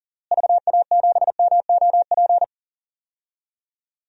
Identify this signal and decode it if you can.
Morse code, heard as a keyed tone, about 40 wpm, 700 Hz. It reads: VA7MOP
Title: DXCC-100 Call Signs - 6 Characters - Repeated 3x - 40wpm